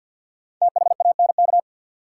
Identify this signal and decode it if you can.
Morse code, heard as a keyed tone, about 51 wpm, 700 Hz. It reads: THANK
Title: Sets of 2 Words - New Word Drill - 50wpm